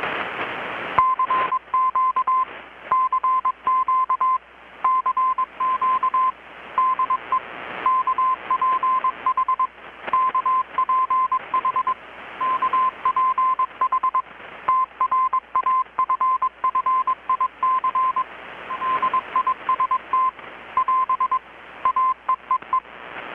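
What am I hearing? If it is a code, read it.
Morse code, heard as a keyed tone, about 21 wpm, 1030 Hz. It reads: CQ CQ CQ DE KPH KPH KPH TRAFFIC DIST <AS> AEEE
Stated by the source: Morse from KPH's Night of Nights XX maritime radiotelegraphy event, received in Chiapas, Mexico about 3,800 km away on a Kenwood R-600 with a grounded 10 m long-wire antenna.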